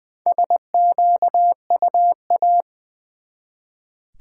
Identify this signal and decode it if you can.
Morse code, heard as a keyed tone, about 20 wpm, 700 Hz. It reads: SQUA
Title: Build A Word - 1 Letter At A Time - 5-Letter Words - 20wpm